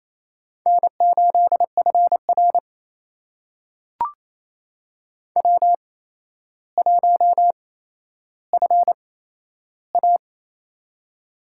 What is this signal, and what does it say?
Morse code, heard as a keyed tone, about 27 wpm, spaced as for 12 wpm, 700 Hz. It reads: N8FR W1FA
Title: USA Call Signs - 4 Characters - 8x Character Spacing - 28wpm